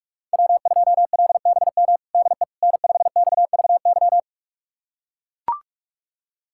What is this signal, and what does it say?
Morse code, heard as a keyed tone, about 42 wpm, 700 Hz. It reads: W2LBM DE N5XVY
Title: USA Callsign DE Callsign 45wpm